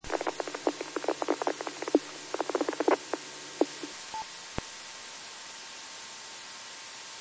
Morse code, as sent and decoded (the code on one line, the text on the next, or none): none